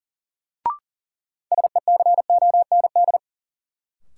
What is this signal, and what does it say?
Morse code, heard as a keyed tone, about 40 wpm, 700 Hz. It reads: SECOND
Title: Sentences from Top 300 Words - New Word Drill - 40wpm